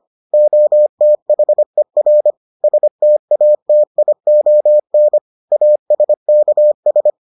Morse code, heard as a keyed tone, about 25 wpm, 600 Hz. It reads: OTHER STATION ASKS